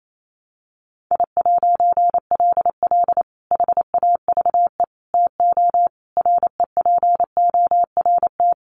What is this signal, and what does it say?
Morse code, heard as a keyed tone, about 28 wpm, 700 Hz. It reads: I'LL HAVE TO REPORT
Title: A Sound Of Thunder 28wpm